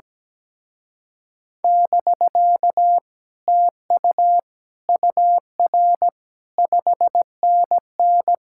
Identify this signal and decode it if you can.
Morse code, heard as a keyed tone, about 17 wpm, 700 Hz. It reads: <BK> TU UR 5NN
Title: Speed-Racing Common QSO Phrases - 17wpm